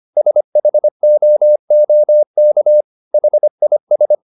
Morse code, heard as a keyed tone, about 25 wpm, 600 Hz. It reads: SHOOK HIS